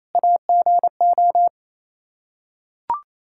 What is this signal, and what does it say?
Morse code, heard as a keyed tone, about 27 wpm, 700 Hz. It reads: AGO
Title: Practice QSO - Exchange 3 - More Info - 28wpm